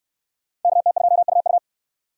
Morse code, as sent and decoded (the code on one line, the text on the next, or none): -.- - ..--- ..- ..-
KT2UU